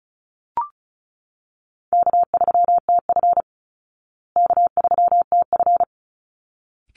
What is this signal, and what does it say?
Morse code, heard as a keyed tone, about 34 wpm, 700 Hz. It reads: K3TF K3TF
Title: Easy Call Signs Repeated 2x - 35wpm